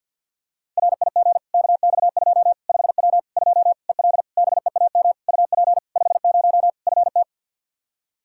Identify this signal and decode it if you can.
Morse code, heard as a keyed tone, about 51 wpm, 700 Hz. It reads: RIG KX2 5W 2 EL BEAM UP 50 FT
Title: Practice QSO - Exchange 2 - Info - 50wpm